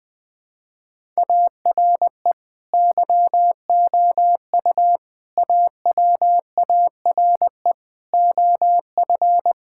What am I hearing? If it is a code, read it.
Morse code, heard as a keyed tone, about 20 wpm, 700 Hz. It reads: ARE YOU AWARE OF
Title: Sentences with 2-Word Phrases - 20wpm